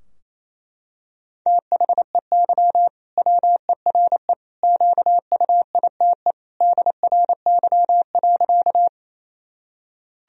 Morse code, heard as a keyed tone, about 28 wpm, 700 Hz. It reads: THEY WERE QUITE DRY.
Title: Sentences from Top 1000 Words - Review Part 3 - 28wpm